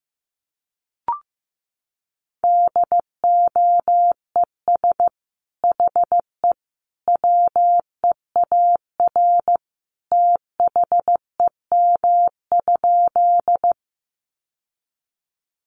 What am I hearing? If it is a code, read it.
Morse code, heard as a keyed tone, about 15 wpm, 700 Hz. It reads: DOES HE WEAR THEM?
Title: Sentences from Top 1500 Words 15wpm